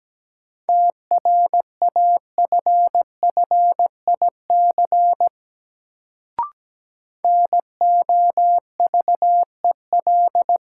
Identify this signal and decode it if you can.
Morse code, heard as a keyed tone, about 17 wpm, 700 Hz. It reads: TRAFFIC NOVEL
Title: Sentences from Top 1300 Words - New Word Drill - 17wpm